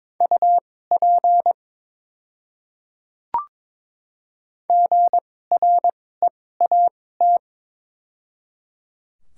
Morse code, heard as a keyed tone, about 22 wpm, spaced as for 16 wpm, 700 Hz. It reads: UP GREAT
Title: 2x Character Spacing - Top 200 Words - 22wpm